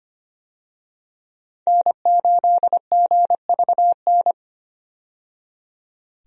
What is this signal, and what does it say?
Morse code, heard as a keyed tone, about 25 wpm, 690 Hz. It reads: N8GVN
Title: USA Call Signs - 5 Characters - 25wpm